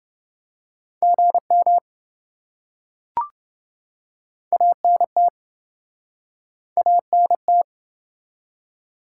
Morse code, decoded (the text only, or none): GM ANT ANT